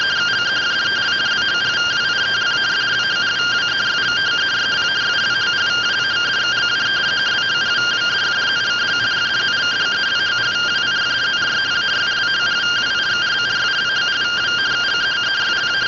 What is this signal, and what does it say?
Two-tone teleprinter signal, about 170 Hz shift.
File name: findings / MR CORONAL 55.gen-16 rtty rx